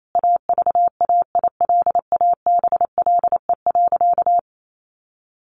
Morse code, heard as a keyed tone, about 28 wpm, 700 Hz. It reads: AVAILABLE.
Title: Sentences from Top 600 Words 28wpm